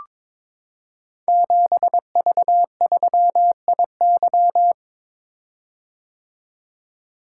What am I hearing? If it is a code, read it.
Morse code, heard as a keyed tone, about 22 wpm, 700 Hz. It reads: 7V3IY
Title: Code Groups - 5 Letter:Numbers - 22wpm